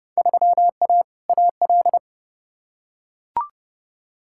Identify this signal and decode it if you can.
Morse code, heard as a keyed tone, about 30 wpm, 700 Hz. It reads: 3A AL
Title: Field Day Exchanges - 30wpm